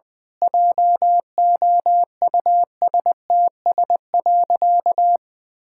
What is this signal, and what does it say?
Morse code, heard as a keyed tone, about 21 wpm, 700 Hz. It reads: JOUSTS.
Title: Pangram Challenge v2 - 20wpm